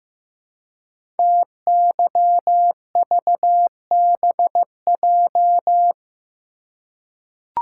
Morse code, heard as a keyed tone, about 15 wpm, 700 Hz. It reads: TYVBJ
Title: Code Groups - 5 Letters - 15wpm